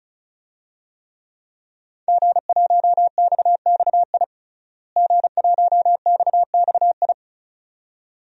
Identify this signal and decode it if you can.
Morse code, heard as a keyed tone, about 36 wpm, 700 Hz. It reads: G1XXI G1XXI
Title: DXCC-100 Call Signs - 5 Characters - Repeated 2x - 35wpm